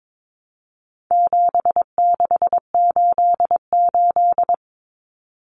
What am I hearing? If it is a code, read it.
Morse code, heard as a keyed tone, about 22 wpm, 700 Hz. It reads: 7688
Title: Common Cut Numbers - 4-Digit 22wpm